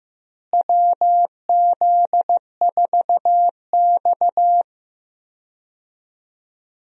Morse code, heard as a keyed tone, about 15 wpm, 700 Hz. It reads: WZ4X
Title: USA Call Signs - 4 Characters - Repeated 3x - 15wpm